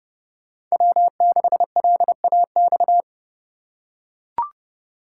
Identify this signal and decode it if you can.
Morse code, heard as a keyed tone, about 30 wpm, 700 Hz. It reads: W6LAX
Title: K1USN SST QSO Practice - 30wpm